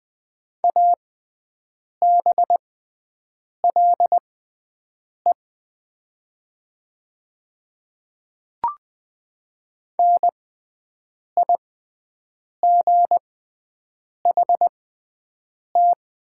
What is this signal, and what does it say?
Morse code, heard as a keyed tone, about 20 wpm, spaced as for 7 wpm, 700 Hz. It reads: ABLE NIGHT
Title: 6x Character Spacing - Top 500 Words - 20wpm